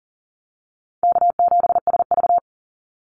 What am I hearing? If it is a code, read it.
Morse code, heard as a keyed tone, about 41 wpm, 700 Hz. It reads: K7SV
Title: Call Signs - 40wpm